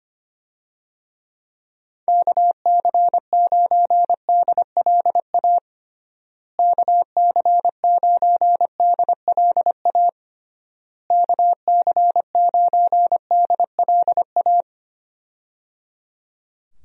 Morse code, heard as a keyed tone, about 25 wpm, 700 Hz. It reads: KC9DLA KC9DLA KC9DLA
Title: USA Call Signs - 6 Characters - Repeated 3x - 25wpm